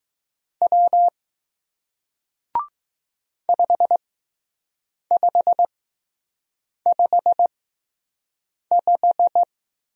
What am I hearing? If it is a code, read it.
Morse code, heard as a keyed tone, about 20 wpm, spaced as for 6 wpm, 700 Hz. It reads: W 5555